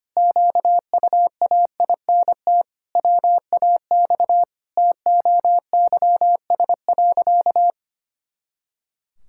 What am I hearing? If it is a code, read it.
Morse code, heard as a keyed tone, about 25 wpm, 700 Hz. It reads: QUAINT WAX TOYS.
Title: Pangram Challenge v2 - 25wpm